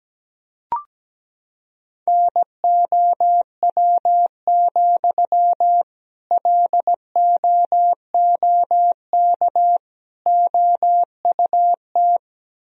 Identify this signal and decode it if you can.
Morse code, heard as a keyed tone, about 17 wpm, 700 Hz. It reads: NOW, LOOK OUT
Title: Sentences from Top 1000 Words - Review Part 4 - 17wpm